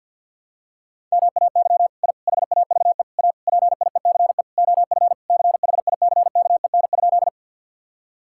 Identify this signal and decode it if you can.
Morse code, heard as a keyed tone, about 48 wpm, 700 Hz. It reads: MAY I HAVE A PIECE OF CHICKEN?
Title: Sentences from Top 1000 Words - Review Part 5 - 50wpm